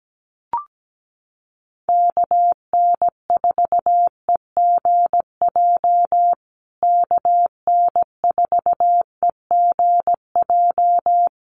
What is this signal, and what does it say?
Morse code, heard as a keyed tone, about 17 wpm, 700 Hz. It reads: KN4EGJ KN4EGJ
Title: USA Call Signs Repeated 3x 17wpm